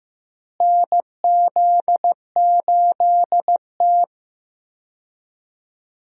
Morse code, heard as a keyed tone, about 15 wpm, 700 Hz. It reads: NZ8T
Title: USA Call Signs - 4 Characters - Repeated 3x - 15wpm